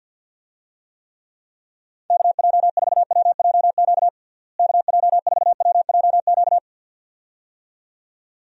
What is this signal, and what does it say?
Morse code, heard as a keyed tone, about 51 wpm, 700 Hz. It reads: KJ4WJQ KJ4WJQ
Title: USA Call Signs - 6 Characters - Repeated 2x - 50wpm